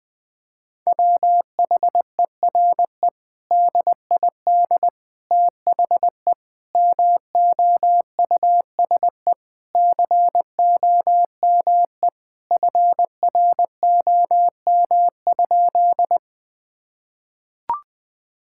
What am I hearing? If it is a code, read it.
Morse code, heard as a keyed tone, about 21 wpm, 700 Hz. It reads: WHERE DID THE MOUSE COME FROM?
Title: Sentences from Top 1000 Words - Review Part 5 - 20wpm